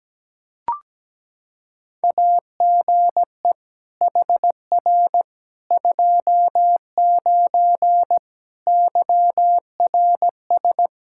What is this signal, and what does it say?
Morse code, heard as a keyed tone, about 17 wpm, 700 Hz. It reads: AGE HR 29 YRS
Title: Practice QSO - Exchange 3 - More Info - 17wpm